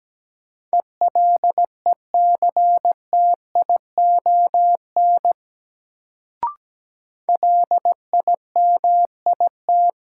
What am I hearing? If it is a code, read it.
Morse code, heard as a keyed tone, about 17 wpm, 700 Hz. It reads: ELECTION LIMIT
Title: Sentences from Top 1500 Words - New Word Drill Alt - 17wpm